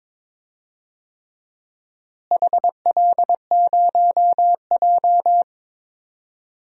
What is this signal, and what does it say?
Morse code, heard as a keyed tone, about 21 wpm, 700 Hz. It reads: HL0J
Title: DXCC-100 Call Signs - 4 Characters - Repeated 3x - 22wpm